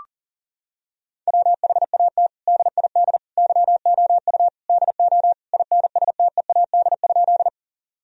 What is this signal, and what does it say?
Morse code, heard as a keyed tone, about 37 wpm, 700 Hz. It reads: WHAT DID YOU DO INSTEAD?
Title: Sentences from Top 1000 Words - Review Part 4 - 40wpm